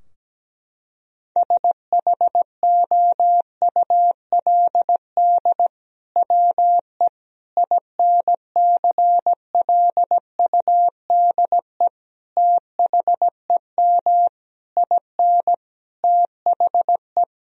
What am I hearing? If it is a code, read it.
Morse code, heard as a keyed tone, about 17 wpm, 700 Hz. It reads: SHOULD WE INCLUDE THEM IN THE
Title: Sentences from Top 1000 Words - Review Part 3 - 17wpm